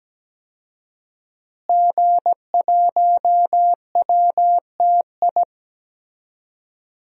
Morse code, heard as a keyed tone, about 17 wpm, 700 Hz. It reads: G1WTI
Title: DXCC-100 Call Signs - 5 Characters - Repeated 2x - 17wpm